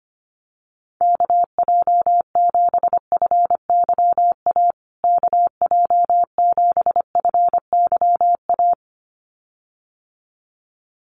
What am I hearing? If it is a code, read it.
Morse code, heard as a keyed tone, about 25 wpm, 700 Hz. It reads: KJ7FYA KJ7FYA
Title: USA Call Signs Repeated 2x 25wpm